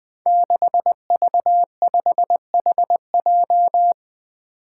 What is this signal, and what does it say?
Morse code, heard as a keyed tone, about 20 wpm, 700 Hz. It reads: 6V5HJ
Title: Five Challenging Letter-Number Combinations 20wpm